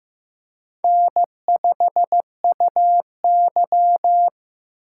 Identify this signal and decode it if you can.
Morse code, heard as a keyed tone, about 15 wpm, 700 Hz. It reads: N5UY
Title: USA Call Signs - 4 Characters - 4x Character Spacing - 15wpm